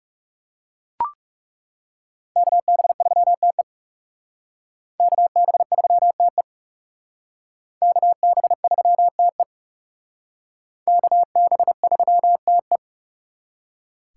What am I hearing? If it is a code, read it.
Morse code, heard as a keyed tone, about 37 wpm, 700 Hz. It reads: KB3TE KB3TE KB3TE KB3TE
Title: Speed Racing - USA 2x2 Call Signs 30wpm